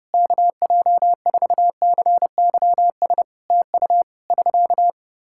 Morse code, heard as a keyed tone, about 32 wpm, 700 Hz. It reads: KJ4CYS TU <SK>
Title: K1USN SST QSO Practice 30wpm